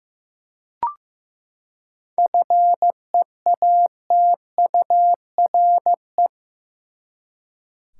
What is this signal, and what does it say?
Morse code, heard as a keyed tone, about 15 wpm, 700 Hz. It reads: FEATURE
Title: Seven Letter Words - 15wpm